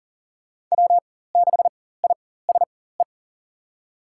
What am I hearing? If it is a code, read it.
Morse code, heard as a keyed tone, about 40 wpm, spaced as for 19 wpm, 700 Hz. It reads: W6ISE
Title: USA Call Signs - 5 Characters - 4x Character Spacing - 40wpm